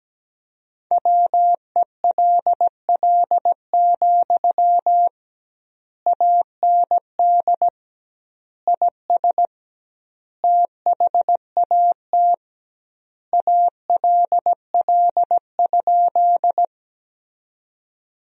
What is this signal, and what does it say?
Morse code, heard as a keyed tone, about 17 wpm, 700 Hz. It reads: WELL, AND IS THAT ALL?
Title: 2x Word-Spacing - Sentences from Top 200 Words - 17wpm